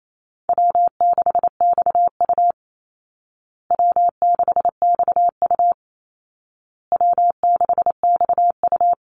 Morse code, heard as a keyed tone, about 28 wpm, 700 Hz. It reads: W6XU W6XU W6XU
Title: Easy Call Signs Repeated 3x - 28wpm